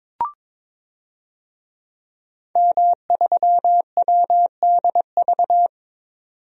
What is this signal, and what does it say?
Morse code, heard as a keyed tone, about 22 wpm, 700 Hz. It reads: M3WDV
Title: DXCC-100 Call Signs - 5 Characters - 22wpm